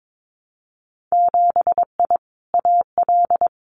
Morse code, heard as a keyed tone, about 22 wpm, 700 Hz. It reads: 7I AL
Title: Winter Field Day Exchanges Encore 22wpm